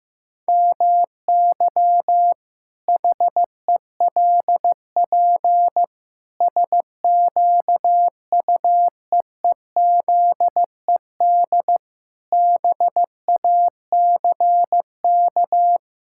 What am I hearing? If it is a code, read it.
Morse code, heard as a keyed tone, about 16 wpm, 700 Hz. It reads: MY HELP SQUEEZED BACK
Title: Pangram Challenge v2 - 15wpm